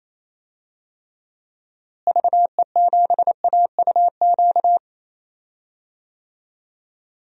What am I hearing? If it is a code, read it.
Morse code, heard as a keyed tone, about 28 wpm, 700 Hz. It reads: VE7AUQ
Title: DXCC-100 Call Signs - 6 Characters - Repeated 3x - 28wpm